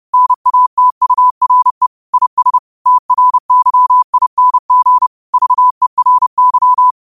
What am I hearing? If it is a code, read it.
Morse code, heard as a keyed tone, about 30 wpm, 1000 Hz. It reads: NATURE IS TRYING VERY
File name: qotdfiles / 30wpm